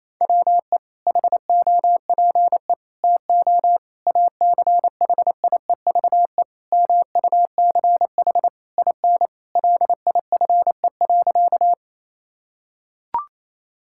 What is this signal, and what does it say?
Morse code, heard as a keyed tone, about 28 wpm, 700 Hz. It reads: WE HOPE TO ACHIEVE MUCH IN LIFE.
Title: Sentences from Top 1000 Words - Review Part 6 - 28wpm